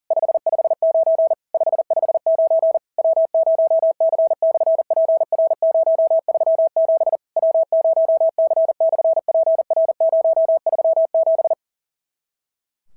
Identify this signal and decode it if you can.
Morse code, heard as a keyed tone, about 43 wpm, 650 Hz. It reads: 559 559 W0C/PR037 W0C/PR037
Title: SOTA QSO Practice 40wpm